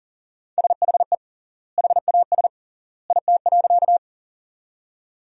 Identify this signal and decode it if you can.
Morse code, heard as a keyed tone, about 40 wpm, 700 Hz. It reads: SHE HAS IT.
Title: 3x Word-Spacing - Sentences from Top 100 Words - 40wpm